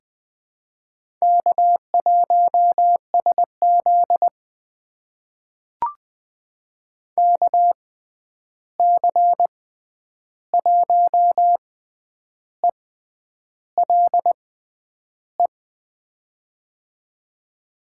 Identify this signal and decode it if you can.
Morse code, heard as a keyed tone, about 20 wpm, spaced as for 9 wpm, 700 Hz. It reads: K1SZ KC1ELE
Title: USA Call Signs - 6x Character Spacing - 20wpm